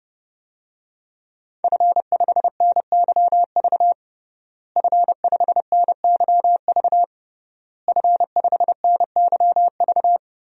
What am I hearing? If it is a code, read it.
Morse code, heard as a keyed tone, about 30 wpm, 700 Hz. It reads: F5NYV F5NYV F5NYV
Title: DXCC-100 Call Signs - 5 Characters - Repeated 3x - 30wpm